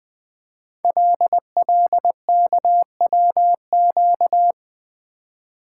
Morse code, heard as a keyed tone, about 20 wpm, 700 Hz. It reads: LLKWQ
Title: Code Groups - 5 Letter:Numbers - 20wpm